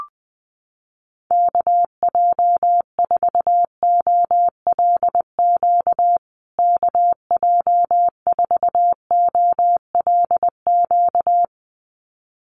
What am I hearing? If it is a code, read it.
Morse code, heard as a keyed tone, about 20 wpm, 700 Hz. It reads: KJ4OLQ KJ4OLQ
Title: USA Call Signs Repeated 2x 20wpm